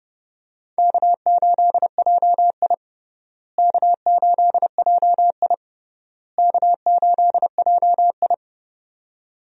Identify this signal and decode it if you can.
Morse code, heard as a keyed tone, about 30 wpm, 700 Hz. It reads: K8JI K8JI K8JI
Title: USA Call Signs - 4 Characters - Repeated 3x - 30wpm